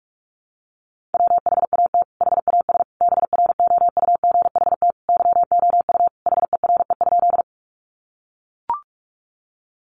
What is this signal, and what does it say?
Morse code, heard as a keyed tone, about 44 wpm, 700 Hz. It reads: WHAT HAS BROUGHT YOU HERE?
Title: Sentences from Top 500 Words 45wpm